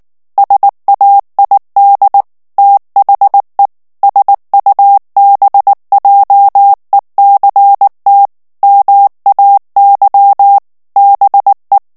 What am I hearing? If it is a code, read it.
Morse code, heard as a keyed tone, about 19 wpm, 800 Hz. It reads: SAID THE SUBJECT MAY BE
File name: ANGLAIS texte clair 3 vitesse 18 mots minute